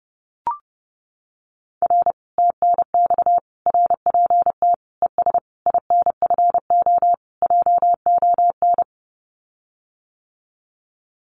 Morse code, heard as a keyed tone, about 30 wpm, 700 Hz. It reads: R TNX RPT ES INFO JON
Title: Common QSO Phrases Encore 30wpm